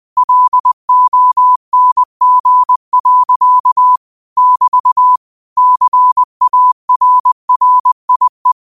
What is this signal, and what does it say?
Morse code, heard as a keyed tone, about 20 wpm, 1000 Hz. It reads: LONG. <BT> CARRIE
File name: qotdfiles / 20wpm